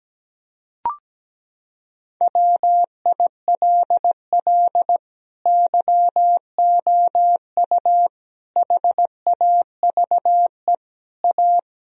Morse code, heard as a keyed tone, about 17 wpm, 700 Hz. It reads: WILL YOU HAVE A
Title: Sentences from Top 1000 Words - Review Part 1 - 17wpm